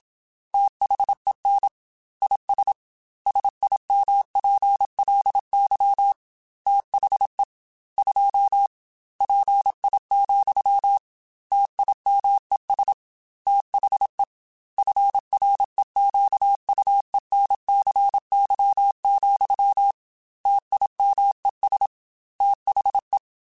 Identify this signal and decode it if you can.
Morse code, heard as a keyed tone, about 27 wpm, 800 Hz. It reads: THEN IS SIMPLY THE 2 PI, TIMES THE FREQUENCY, TIMES THE